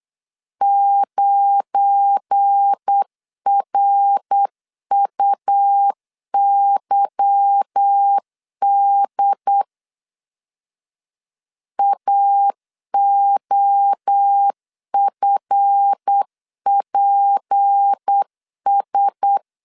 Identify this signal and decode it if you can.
Morse code, heard as a keyed tone, about 8 wpm, 800 Hz. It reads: 9RUYD AOFPS
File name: LEZIONI / 8 YJ45 / 8 Y 8wpm 3245